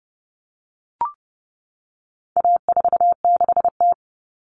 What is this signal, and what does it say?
Morse code, heard as a keyed tone, about 30 wpm, 700 Hz. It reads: A46T
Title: Common Cut Numbers - 4-Digit 30wpm